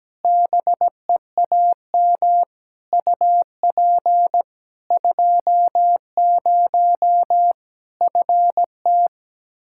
Morse code, heard as a keyed tone, about 17 wpm, 700 Hz. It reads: BEAM UP 20 FT
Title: Practice QSO - Exchange 2 - Info - 17wpm